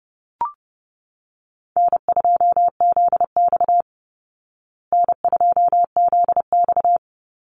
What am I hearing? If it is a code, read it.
Morse code, heard as a keyed tone, about 30 wpm, 700 Hz. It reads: N2ZX N2ZX
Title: Easy Call Signs Repeated 2x - 30wpm